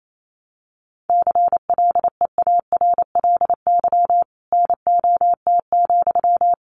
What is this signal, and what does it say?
Morse code, heard as a keyed tone, about 28 wpm, 700 Hz. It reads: CLEARLY NOT,
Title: Sentences from Top 1300 Words 28wpm